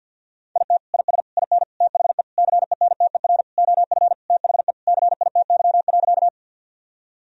Morse code, heard as a keyed tone, about 52 wpm, 700 Hz. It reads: IT IS IN THE CENTER OF THE CITY.
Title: Sentences from Top 1000 Words - Review Part 2 - 50wpm